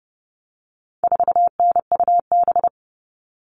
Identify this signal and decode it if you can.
Morse code, heard as a keyed tone, about 30 wpm, 700 Hz. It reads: 4NUB
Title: All Cut Numbers - 4-Digit 30wpm